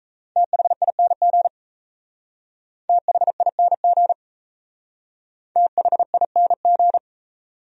Morse code, heard as a keyed tone, about 34 wpm, 700 Hz. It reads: THING THING THING
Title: Speed-Racing Top 500 Words 28wpm